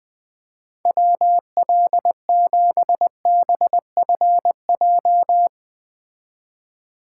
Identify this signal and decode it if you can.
Morse code, heard as a keyed tone, about 21 wpm, 700 Hz. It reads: WL7BFJ
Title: Speed Racing - USA 2x3 Call Signs 17wpm